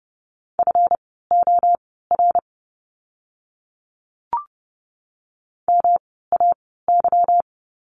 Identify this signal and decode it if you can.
Morse code, heard as a keyed tone, about 30 wpm, spaced as for 17 wpm, 700 Hz. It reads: FOR MAY
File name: course / 30:10wpm / 230 Lesson 37f - Letter x - Words 30wpm